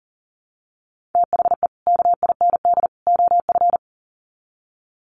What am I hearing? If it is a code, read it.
Morse code, heard as a keyed tone, about 40 wpm, 700 Hz. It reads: THE KIND OF
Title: Sets of 3 Words 40wpm